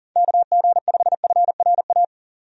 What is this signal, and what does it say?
Morse code, heard as a keyed tone, about 40 wpm, 700 Hz. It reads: KG5FRA
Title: USA Call Signs - 6 Characters - 40wpm